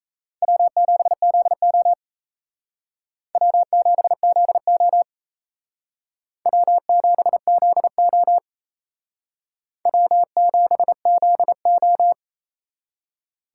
Morse code, heard as a keyed tone, about 37 wpm, 700 Hz. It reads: W7ZO W7ZO W7ZO W7ZO